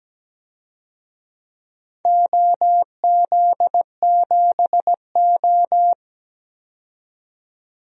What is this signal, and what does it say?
Morse code, heard as a keyed tone, about 17 wpm, 700 Hz. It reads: OZ7O
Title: DXCC-100 Call Signs - 4 Characters - 17wpm